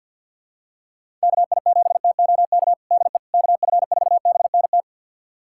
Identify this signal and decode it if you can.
Morse code, heard as a keyed tone, about 52 wpm, 700 Hz. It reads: KI7TOK DE KF4BNT
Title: USA Callsign DE Callsign - 6 Characters - 50wpm